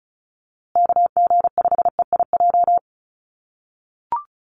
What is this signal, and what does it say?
Morse code, heard as a keyed tone, about 34 wpm, 700 Hz. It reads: KG5EIJ
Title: USA Call Signs Repeated 3x 35wpm